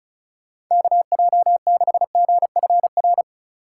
Morse code, heard as a keyed tone, about 46 wpm, 700 Hz. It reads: KJ6GFR